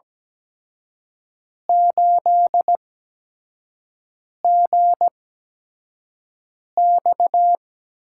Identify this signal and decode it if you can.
Morse code, heard as a keyed tone, about 17 wpm, spaced as for 5 wpm, 700 Hz. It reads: 8GX
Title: USA Call Signs - 8x Character Spacing - 17wpm